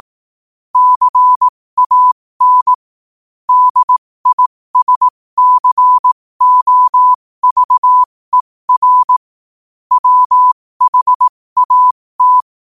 Morse code, heard as a keyed tone, about 18 wpm, spaced as for 15 wpm, 1000 Hz. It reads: CAN DISCOVER WHAT
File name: qotdfiles / 15wpm